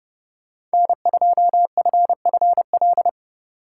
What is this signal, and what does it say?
Morse code, heard as a keyed tone, about 29 wpm, 700 Hz. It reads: N2FFL